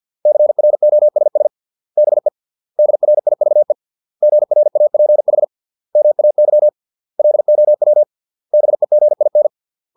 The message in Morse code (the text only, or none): CROSS BE DRIVE GRAPH MAY LOW BEGIN